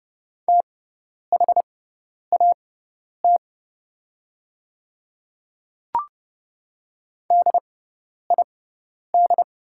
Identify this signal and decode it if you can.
Morse code, heard as a keyed tone, about 30 wpm, spaced as for 10 wpm, 700 Hz. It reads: THAT DID
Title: 6x Character Spacing - Top 100 Words - 30wpm